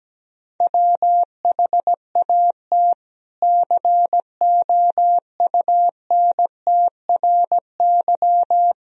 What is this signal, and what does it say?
Morse code, heard as a keyed tone, about 17 wpm, 700 Hz. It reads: WHAT COUNTRY
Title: Sentences from Top 300 Words and Abbreviations 17wpm